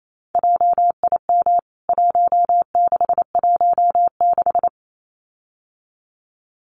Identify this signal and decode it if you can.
Morse code, heard as a keyed tone, about 28 wpm, 700 Hz. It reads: JIM 1616
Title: CWT Exchange 28wpm